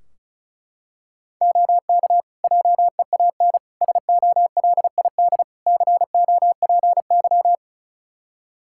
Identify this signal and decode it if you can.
Morse code, heard as a keyed tone, about 36 wpm, 700 Hz. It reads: OK JEAN SOLID COPY
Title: Practice QSO - Exchange 3 - More Info - 35wpm